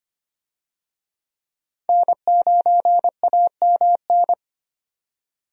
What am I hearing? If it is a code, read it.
Morse code, heard as a keyed tone, about 25 wpm, 700 Hz. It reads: N9AMN
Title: USA Call Signs - 5 Characters - 25wpm